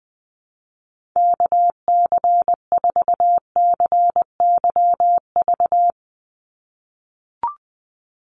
Morse code, heard as a keyed tone, about 20 wpm, 700 Hz. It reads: KC4CYV
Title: USA Call Signs 20wpm